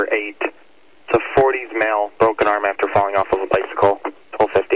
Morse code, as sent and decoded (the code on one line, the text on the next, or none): none